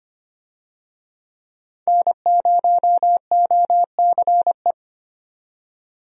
Morse code, heard as a keyed tone, about 25 wpm, 700 Hz. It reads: N0OCE
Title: USA Call Signs - 5 Characters - 25wpm